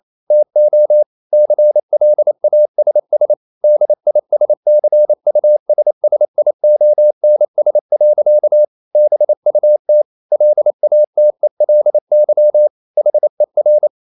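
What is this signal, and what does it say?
Morse code, heard as a keyed tone, about 28 wpm, 600 Hz. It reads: TO CLASS DISCUSSIONS. BUT LATELY HER
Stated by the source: Morse code practice at 28 wpm, text of Ken Follett's novel Never in English, from CW med Gnister.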